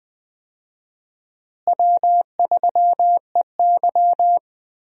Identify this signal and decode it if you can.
Morse code, heard as a keyed tone, about 20 wpm, 700 Hz. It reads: W3EY
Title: USA Call Signs - 4 Characters - Repeated 2x - 20wpm